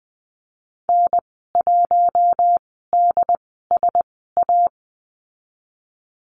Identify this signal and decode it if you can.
Morse code, heard as a keyed tone, about 20 wpm, spaced as for 14 wpm, 700 Hz. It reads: N1DSA
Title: 088 Lesson 15g - Number 5 - Call Signs 20wpm